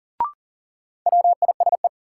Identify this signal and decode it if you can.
Morse code, heard as a keyed tone, about 40 wpm, 700 Hz. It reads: WISE
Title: Sentences from Top 1600 Words - New Word Drill Alt - 40wpm